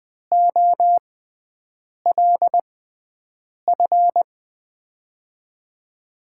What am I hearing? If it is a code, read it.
Morse code, heard as a keyed tone, about 20 wpm, spaced as for 7 wpm, 700 Hz. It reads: OLF
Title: USA Call Signs - 5 Characters - 6x Character Spacing - 20wpm